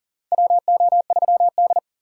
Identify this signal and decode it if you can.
Morse code, heard as a keyed tone, about 40 wpm, 700 Hz. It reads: WO3D
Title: USA Call Signs - 4 Characters - 6x Character Spacing - 40wpm